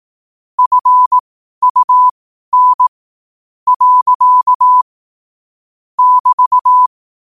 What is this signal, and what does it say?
Morse code, heard as a keyed tone, about 18 wpm, spaced as for 11 wpm, 1000 Hz. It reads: FUN. <BT>
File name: qotdfiles / 12wpm